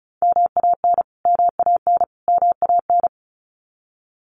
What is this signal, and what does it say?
Morse code, heard as a keyed tone, about 34 wpm, 700 Hz. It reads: MAN MAN MAN
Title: Top 300 Words Repeated 35wpm